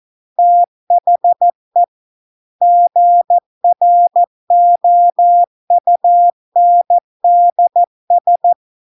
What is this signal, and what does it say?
Morse code, heard as a keyed tone, about 14 wpm, 700 Hz. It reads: THE GROUNDS